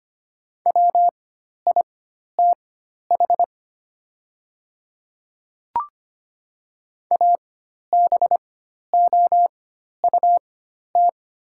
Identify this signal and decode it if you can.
Morse code, heard as a keyed tone, about 25 wpm, spaced as for 12 wpm, 700 Hz. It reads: WITH ABOUT
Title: 4x Character Spacing - Top 100 Words - 25wpm